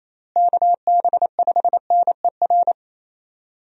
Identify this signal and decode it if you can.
Morse code, heard as a keyed tone, about 28 wpm, 700 Hz. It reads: KB5NER
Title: USA Call Signs - 6 Characters - Repeated 2x - 28wpm